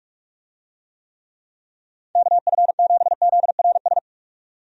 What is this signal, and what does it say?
Morse code, heard as a keyed tone, about 46 wpm, 700 Hz. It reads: KF7ZRS
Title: USA Call Signs - 6 Characters - 45wpm